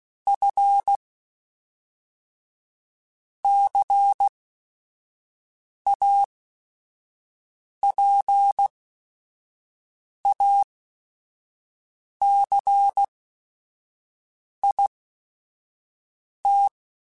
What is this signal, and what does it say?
Morse code, heard as a keyed tone, about 16 wpm, spaced as for 5 wpm, 800 Hz. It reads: F CAPACIT
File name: Technical Subjects / Technical Subjects-mp3s / XC @5